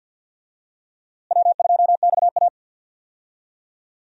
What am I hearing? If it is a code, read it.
Morse code, heard as a keyed tone, about 52 wpm, 700 Hz. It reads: W2XA
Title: USA Call Signs - 4 Characters - Repeated 2x - 50wpm